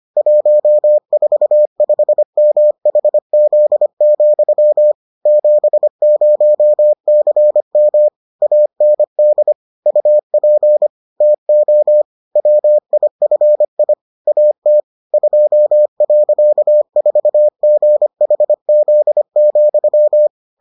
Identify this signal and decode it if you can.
Morse code, heard as a keyed tone, about 25 wpm, 600 Hz. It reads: 145MHZ, 70CM AND UP TO WIFI AT 2.4GHZ,